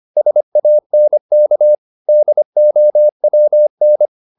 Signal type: Morse code